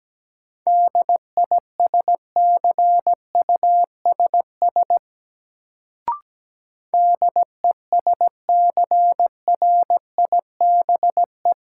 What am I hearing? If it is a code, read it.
Morse code, heard as a keyed tone, about 17 wpm, 700 Hz. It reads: DISCUSS DESCRIBE
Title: Sentences from Top 1000 Words - New Word Drill - 17wpm